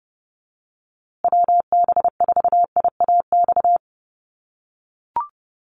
Morse code, heard as a keyed tone, about 30 wpm, 700 Hz. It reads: WB4IAX
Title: USA Call Signs Repeated 3x 30wpm